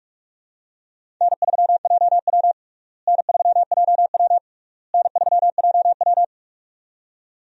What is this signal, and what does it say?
Morse code, heard as a keyed tone, about 46 wpm, 700 Hz. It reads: N3JW N3JW N3JW
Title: USA Call Signs - 4 Characters - Repeated 3x - 45wpm